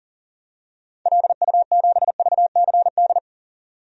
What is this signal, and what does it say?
Morse code, heard as a keyed tone, about 40 wpm, 700 Hz. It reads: LU7VCD
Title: DXCC-100 Call Signs - 6 Characters - 40wpm